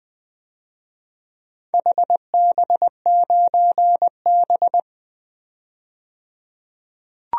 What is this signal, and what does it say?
Morse code, heard as a keyed tone, about 20 wpm, 700 Hz. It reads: HB9B
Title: DXCC-100 Call Signs - 4 Characters - 20wpm